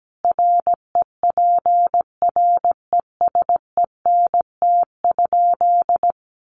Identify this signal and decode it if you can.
Morse code, heard as a keyed tone, about 17 wpm, 700 Hz. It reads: REPRESENT?
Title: Sentences from Top 1200 Words 17wpm